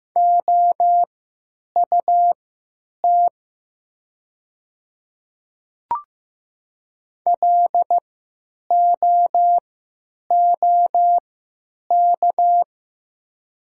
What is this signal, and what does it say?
Morse code, heard as a keyed tone, about 15 wpm, spaced as for 9 wpm, 700 Hz. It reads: OUT LOOK
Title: 3x Character Spacing - Top 100 Words - 15wpm